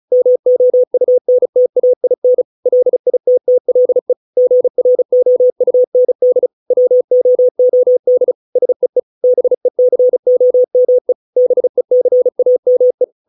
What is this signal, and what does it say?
Morse code, heard as a keyed tone, about 37 wpm, 500 Hz. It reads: MOUNTAIN LITTLE GROUND WOOD SEE BECOME BECAME